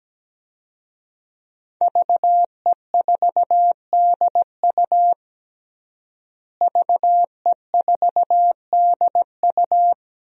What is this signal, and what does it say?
Morse code, heard as a keyed tone, about 17 wpm, 700 Hz. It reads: VE4DU VE4DU
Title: DXCC-100 Call Signs - 5 Characters - Repeated 2x - 17wpm